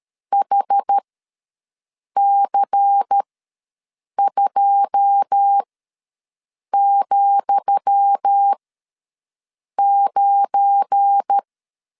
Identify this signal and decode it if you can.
Morse code, heard as a keyed tone, about 13 wpm, spaced as for 6 wpm, 800 Hz. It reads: HC2,9
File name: LEZIONI / 10 SIGNS / 10D COMA ws 12wpm 3618